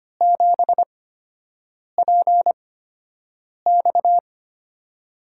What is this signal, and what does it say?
Morse code, heard as a keyed tone, about 25 wpm, spaced as for 7 wpm, 700 Hz. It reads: 7PX
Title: USA Call Signs - 8x Character Spacing - 25wpm